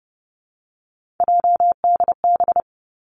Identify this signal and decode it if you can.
Morse code, heard as a keyed tone, about 32 wpm, 700 Hz. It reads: JDB
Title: Three Challenging Letter-Number Combinations 30wpm